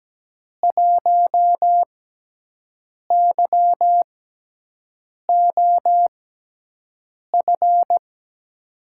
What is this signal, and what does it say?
Morse code, heard as a keyed tone, about 17 wpm, spaced as for 6 wpm, 700 Hz. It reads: 1YOF
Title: USA Call Signs - 6 Characters - 6x Character Spacing - 17wpm